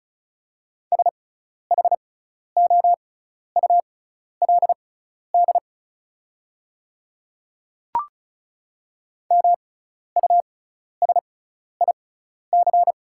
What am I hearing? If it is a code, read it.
Morse code, heard as a keyed tone, about 36 wpm, spaced as for 12 wpm, 700 Hz. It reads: SHOULD MUSIC
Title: 6x Character Spacing - Top 500 Words - 35wpm